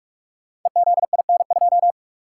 Morse code, heard as a keyed tone, about 44 wpm, 700 Hz. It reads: EZIN2
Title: Code Groups - 5 Letter:Numbers - 45wpm